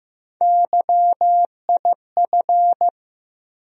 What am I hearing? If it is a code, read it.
Morse code, heard as a keyed tone, about 15 wpm, 700 Hz. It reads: YIF
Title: USA Call Sign Postfixes - 3 Letters 15wpm